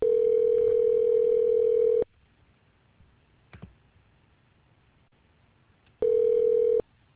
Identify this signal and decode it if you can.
Ringback tone (North America), 440 + 480 Hz.